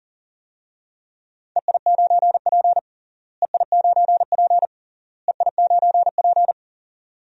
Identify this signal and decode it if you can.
Morse code, heard as a keyed tone, about 40 wpm, 700 Hz. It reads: EI9P EI9P EI9P
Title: DXCC-100 Call Signs - 4 Characters - Repeated 3x - 40wpm